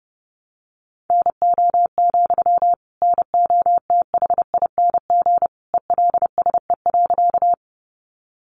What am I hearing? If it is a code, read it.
Morse code, heard as a keyed tone, about 30 wpm, 700 Hz. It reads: NO, NOTHING ELSE.
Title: Sentences from Top 800 Words 30wpm